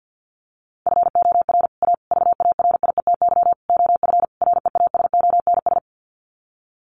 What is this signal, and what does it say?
Morse code, heard as a keyed tone, about 50 wpm, 700 Hz. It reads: FOR A VARIETY OF REASONS
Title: Sets of 5 Words Encore 50wpm